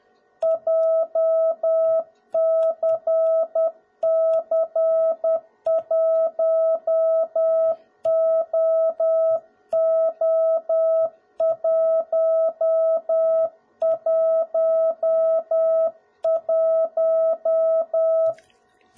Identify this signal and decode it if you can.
Morse code, heard as a keyed tone, about 10 wpm, 650 Hz. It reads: JCC1OO111
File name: jh8cbh / 2021cw / cw01b